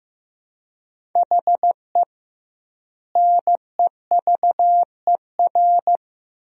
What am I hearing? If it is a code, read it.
Morse code, heard as a keyed tone, about 16 wpm, 700 Hz. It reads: HE NEVER
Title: 2x Word-Spacing - Sentences from Top 400 Words - 15wpm